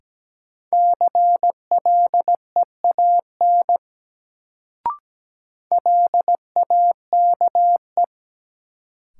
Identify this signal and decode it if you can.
Morse code, heard as a keyed tone, about 17 wpm, 700 Hz. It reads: CLEAN LAKE
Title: Sentences from Top 1100 Words - New Word Drill Alt - 17wpm